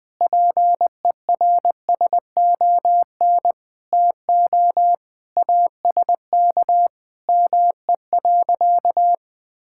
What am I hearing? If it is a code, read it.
Morse code, heard as a keyed tone, about 20 wpm, 700 Hz. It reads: PERSON TO ASK ME.
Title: Sentences from Top 1000 Words - Review Part 5 - 20wpm